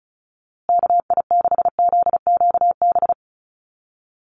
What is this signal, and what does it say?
Morse code, heard as a keyed tone, about 34 wpm, 700 Hz. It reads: KI6ZQB
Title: USA Call Signs Repeated 2x 35wpm